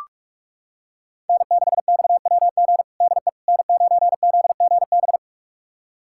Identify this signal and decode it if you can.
Morse code, heard as a keyed tone, about 43 wpm, 700 Hz. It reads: N6XWG DE N9ZGB
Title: USA Callsign DE Callsign - 5 Characters - 45wpm